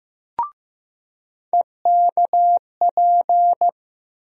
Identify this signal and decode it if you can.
Morse code, heard as a keyed tone, about 15 wpm, 700 Hz. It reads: EKP
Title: USA Call Sign Postfixes - 3 Letters 15wpm